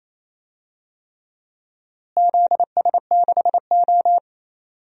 Morse code, heard as a keyed tone, about 28 wpm, 700 Hz. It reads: ZS6O
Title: DXCC-100 Call Signs - 4 Characters - Repeated 3x - 28wpm